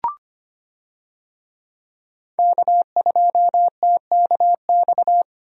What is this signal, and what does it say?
Morse code, heard as a keyed tone, about 25 wpm, 690 Hz. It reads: K2TKX
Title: USA Call Signs - 5 Characters - 25wpm